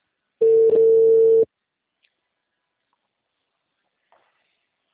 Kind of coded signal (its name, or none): telephone call progress tone